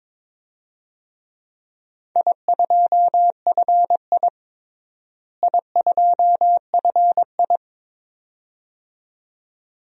Morse code, heard as a keyed tone, about 22 wpm, 700 Hz. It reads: I2FI I2FI
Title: DXCC-100 Call Signs - 4 Characters - Repeated 2x - 22wpm